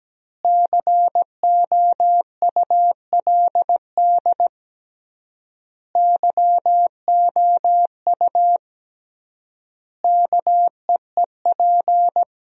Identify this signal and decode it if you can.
Morse code, heard as a keyed tone, about 17 wpm, 700 Hz. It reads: COULD YOU KEEP
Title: 3x Word-Spacing - Sentences from Top 300 Words - 17wpm